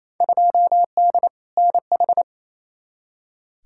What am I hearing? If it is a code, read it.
Morse code, heard as a keyed tone, about 28 wpm, 700 Hz. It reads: 2D NH
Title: Field Day Exchanges Encore - 28wpm